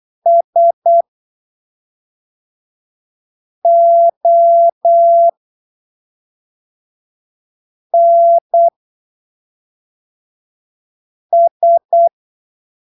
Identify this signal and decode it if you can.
Morse code, heard as a keyed tone, about 8 wpm, spaced as for 3 wpm, 680 Hz. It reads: SONS
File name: arquivos / texto / texto 18 08ppm